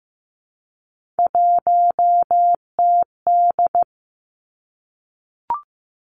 Morse code, heard as a keyed tone, about 15 wpm, 700 Hz. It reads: 1TD
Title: Three Challenging Letter-Number Combinations 15wpm